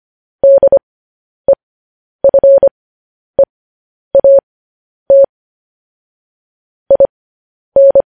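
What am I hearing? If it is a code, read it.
Morse code, heard as a keyed tone, about 25 wpm, spaced as for 10 wpm, 550 Hz. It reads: DEFEAT IN
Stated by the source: Morse code news headlines broadcast, opening with vvv.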